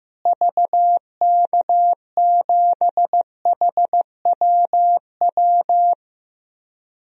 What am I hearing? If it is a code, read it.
Morse code, heard as a keyed tone, about 15 wpm, 700 Hz. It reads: VK7HWW
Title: DXCC-100 Call Signs - 6 Characters - Repeated 3x - 15wpm